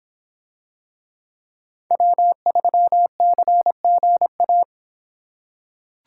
Morse code, heard as a keyed tone, about 24 wpm, 700 Hz. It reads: W3CGA